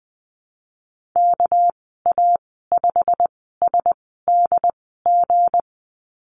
Morse code, heard as a keyed tone, about 19 wpm, spaced as for 14 wpm, 700 Hz. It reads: KA5SDG
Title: 207 Lesson 33g - Letter j - Call Signs 20wpm